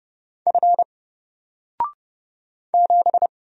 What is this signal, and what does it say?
Morse code, heard as a keyed tone, about 30 wpm, 700 Hz. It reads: F 7
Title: Single Letter-Number 30wpm